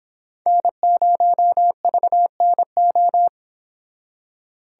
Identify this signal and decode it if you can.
Morse code, heard as a keyed tone, about 26 wpm, 700 Hz. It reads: N0VNO